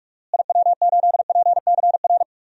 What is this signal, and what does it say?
Morse code, heard as a keyed tone, about 44 wpm, 700 Hz. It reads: IW8PCR
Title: DXCC-100 Call Signs - 6 Characters - Repeated 3x - 45wpm